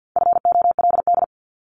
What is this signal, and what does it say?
Morse code, heard as a keyed tone, about 50 wpm, 700 Hz. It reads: FOLD
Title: Multi-letter Phonemes Repeated 3x 50wpm